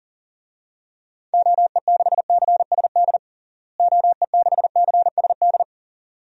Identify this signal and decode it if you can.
Morse code, heard as a keyed tone, about 40 wpm, 700 Hz. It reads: OE6CSD OE6CSD
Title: DXCC-100 Call Signs - 6 Characters - Repeated 2x - 40wpm